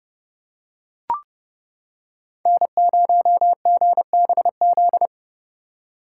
Morse code, heard as a keyed tone, about 29 wpm, 700 Hz. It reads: N0GBZ